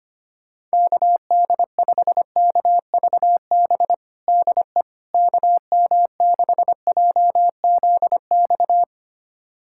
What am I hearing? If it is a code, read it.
Morse code, heard as a keyed tone, about 25 wpm, 700 Hz. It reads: KD5KVB DE KM6JZX
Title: USA Callsign DE Callsign - 6 Characters - 25wpm